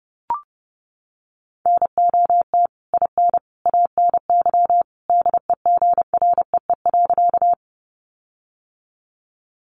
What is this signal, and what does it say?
Morse code, heard as a keyed tone, about 30 wpm, 700 Hz. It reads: NOT IN ANY DEGREE.
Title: Sentences from Top 1500 Words 30wpm